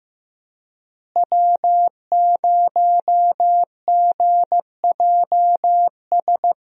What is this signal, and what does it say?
Morse code, heard as a keyed tone, about 15 wpm, 700 Hz. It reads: W0GJS
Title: USA Call Signs - 5 Characters - Repeated 3x - 15wpm